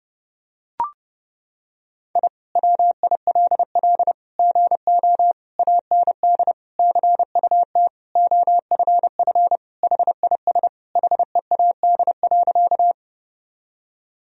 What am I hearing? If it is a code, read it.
Morse code, heard as a keyed tone, about 30 wpm, 700 Hz. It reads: I WILL GO AND CUT OFF HIS HEAD.
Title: Sentences from Top 1000 Words - Review Part 1 - 30wpm